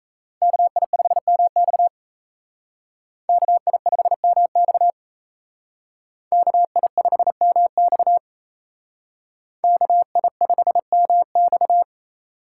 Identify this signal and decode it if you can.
Morse code, heard as a keyed tone, about 37 wpm, 700 Hz. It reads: KI5MX KI5MX KI5MX KI5MX